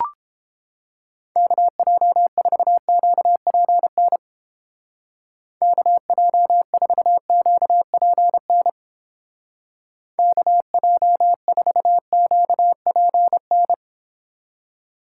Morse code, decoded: KJ4QPN KJ4QPN KJ4QPN

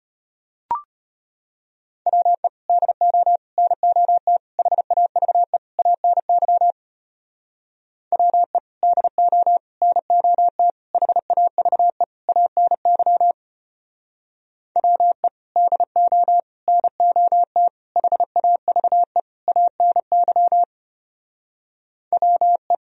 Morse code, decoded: WE DO NOT HAVE ANY WE DO NOT HAVE ANY WE DO NOT HAVE ANY WE

